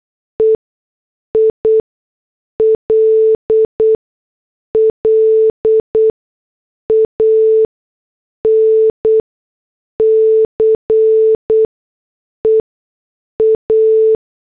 Morse code, heard as a keyed tone, about 8 wpm, spaced as for 6 wpm, 440 Hz. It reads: EILLANCE A
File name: morse / headlines